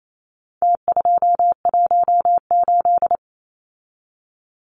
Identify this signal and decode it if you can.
Morse code, heard as a keyed tone, about 28 wpm, 700 Hz. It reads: T218
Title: Numbers - Contest Style - 4-Digit 28wpm